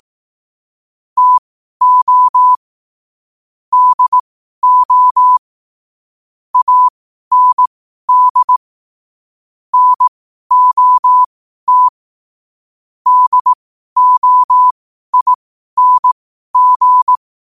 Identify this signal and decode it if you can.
Morse code, heard as a keyed tone, about 18 wpm, spaced as for 12 wpm, 1000 Hz. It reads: TO DO AND NOT DOING
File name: qotdfiles / 12wpm